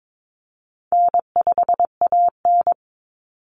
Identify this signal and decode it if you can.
Morse code, heard as a keyed tone, about 22 wpm, 700 Hz. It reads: N5AN
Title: Call Signs Repeated 2x - 22wpm